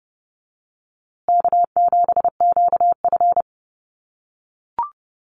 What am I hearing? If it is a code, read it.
Morse code, heard as a keyed tone, about 29 wpm, 700 Hz. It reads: K7QF
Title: USA Call Signs Repeated 2x 30wpm